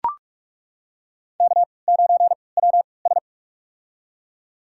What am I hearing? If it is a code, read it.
Morse code, heard as a keyed tone, about 42 wpm, spaced as for 25 wpm, 700 Hz. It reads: K9WS
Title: USA Call Signs - 4 Characters - 3x Character Spacing - 45wpm